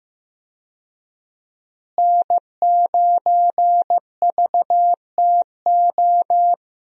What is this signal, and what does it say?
Morse code, heard as a keyed tone, about 15 wpm, 700 Hz. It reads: N9VTO
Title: USA Call Signs - 5 Characters - Repeated 3x - 15wpm